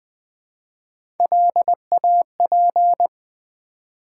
Morse code, heard as a keyed tone, about 20 wpm, 700 Hz. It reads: LAP
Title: Build A Word - 1 Letter At A Time - 3-Letter Words - 20wpm